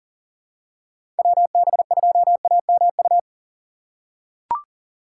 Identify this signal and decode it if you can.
Morse code, heard as a keyed tone, about 40 wpm, 700 Hz. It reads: WB2AMU
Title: USA Call Signs - 6 Characters - 6x Character Spacing - 40wpm